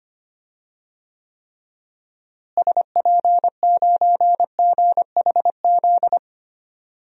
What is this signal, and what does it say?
Morse code, heard as a keyed tone, about 25 wpm, 700 Hz. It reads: SP9GHZ
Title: DXCC-100 Call Signs - 6 Characters - Repeated 2x - 25wpm